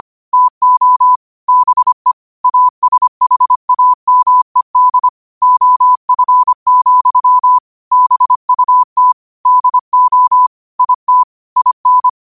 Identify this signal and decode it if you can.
Morse code, heard as a keyed tone, about 25 wpm, 1000 Hz. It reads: TO BE ASHAMED OF, BUT DO IT IN
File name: qotdfiles / 25wpm